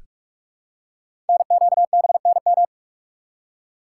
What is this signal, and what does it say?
Morse code, heard as a keyed tone, about 44 wpm, 700 Hz. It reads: NQBNM
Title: Code Groups - 5 Letters - 45wpm